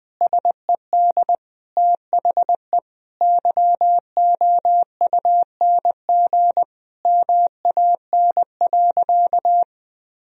Morse code, heard as a keyed tone, about 20 wpm, 700 Hz. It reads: SED THE YOUNG MAN.
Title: Sentences from Top 400 Words and Abbreviations 20wpm